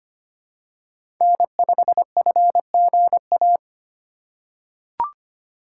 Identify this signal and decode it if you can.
Morse code, heard as a keyed tone, about 25 wpm, 700 Hz. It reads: N5FGA
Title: USA Call Signs - 4x Character Spacing - 25wpm